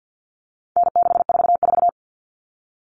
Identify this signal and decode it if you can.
Morse code, heard as a keyed tone, about 50 wpm, 700 Hz. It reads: N644
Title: All Cut Numbers - 4-Digit 50wpm